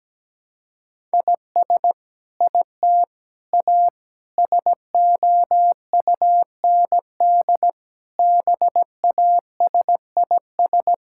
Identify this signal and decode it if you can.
Morse code, heard as a keyed tone, about 17 wpm, 700 Hz. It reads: IS IT A SOUND BASIS
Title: Sentences from Top 1000 Words - Review Part 5 - 17wpm